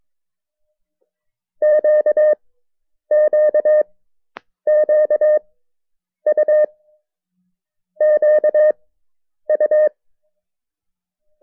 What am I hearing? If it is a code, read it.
Morse code, heard as a keyed tone, about 23 wpm, 590 Hz. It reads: Q Q Q U Q U
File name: FreeFiles / CW-practice / KB6MT-tape1-side2